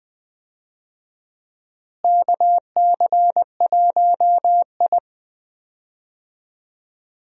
Morse code, heard as a keyed tone, about 21 wpm, 700 Hz. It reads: KC1I
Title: USA Call Signs - 4 Characters - Repeated 3x - 20wpm